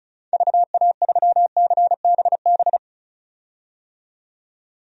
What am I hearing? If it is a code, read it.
Morse code, heard as a keyed tone, about 36 wpm, 700 Hz. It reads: VA3CBB
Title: DXCC-100 Call Signs - 6 Characters - 35wpm